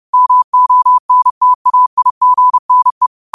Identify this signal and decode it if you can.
Morse code, heard as a keyed tone, about 30 wpm, 1000 Hz. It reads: MONTAIGNE